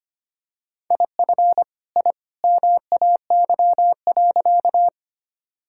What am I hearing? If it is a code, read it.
Morse code, heard as a keyed tone, about 25 wpm, 700 Hz. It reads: IF I MAY.
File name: practice / 25wpm / Sentences with 2-Word Phrases - 25wpm